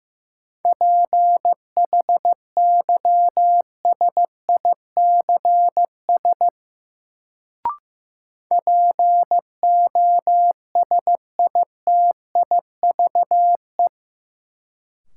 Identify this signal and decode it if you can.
Morse code, heard as a keyed tone, about 15 wpm, 700 Hz. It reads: PHYSICS POSITIVE
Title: Sentences from Top 1000 Words - Review Part 5 - New Word Drill - 15wpm